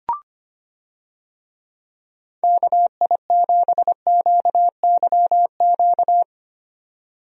Morse code, heard as a keyed tone, about 25 wpm, 700 Hz. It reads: KI7QYQ
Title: USA Call Signs - 6 Characters - 25wpm